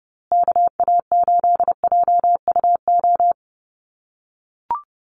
Morse code, heard as a keyed tone, about 30 wpm, 700 Hz. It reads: KA8JUO
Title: USA Call Signs Repeated 2x 30wpm